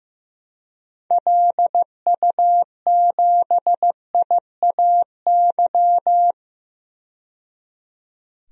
Morse code, heard as a keyed tone, about 15 wpm, 700 Hz. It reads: LU7IAY
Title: DXCC-100 Call Signs - 6 Characters - 15wpm